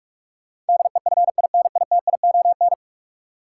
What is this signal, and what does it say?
Morse code, heard as a keyed tone, about 46 wpm, 700 Hz. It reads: DEFINITION
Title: Sentences from Top 1000 Words - Review Part 4 - New Word Drill - 45wpm